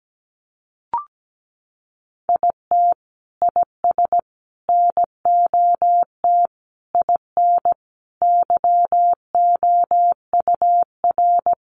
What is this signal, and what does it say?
Morse code, heard as a keyed tone, about 17 wpm, 700 Hz. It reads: IT IS NOT IN YOUR
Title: Sentences from Top 1400 Words 17wpm